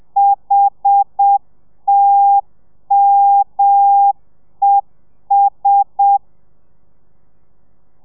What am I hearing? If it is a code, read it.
Morse code, heard as a keyed tone, about 7 wpm, 800 Hz. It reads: HTMES
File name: telegrafia / i2tfj / Lez1 30